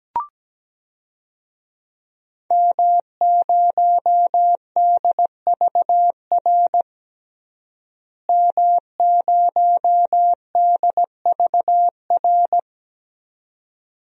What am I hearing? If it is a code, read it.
Morse code, heard as a keyed tone, about 17 wpm, 700 Hz. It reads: M0DVR M0DVR
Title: DXCC-100 Call Signs - 5 Characters - Repeated 2x - 17wpm